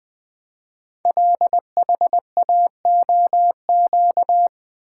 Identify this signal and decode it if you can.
Morse code, heard as a keyed tone, about 20 wpm, 700 Hz. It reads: LHAOQ
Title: Code Groups - 5 Letters - 20wpm